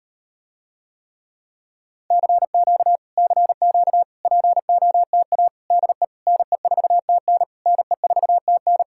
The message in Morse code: CQ CQ POTA DE NE4TN NE4TN

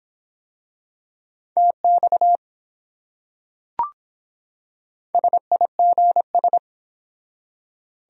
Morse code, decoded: TX SIGS

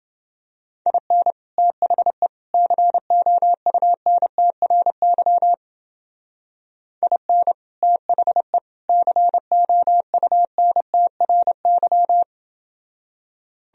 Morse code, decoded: IN THE COUNTRY IN THE COUNTRY

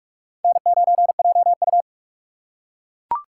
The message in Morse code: N9JU

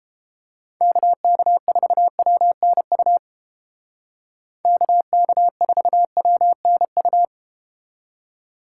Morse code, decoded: KK4WNU KK4WNU

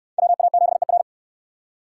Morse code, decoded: WA7ER

